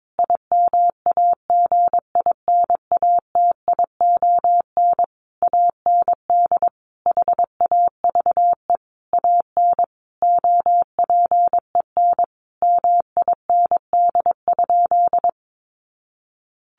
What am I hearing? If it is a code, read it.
Morse code, heard as a keyed tone, about 22 wpm, 700 Hz. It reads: IMAGINATION AND HAVE AN OPEN MIND?
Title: Sentences from Top 1300 Words 22wpm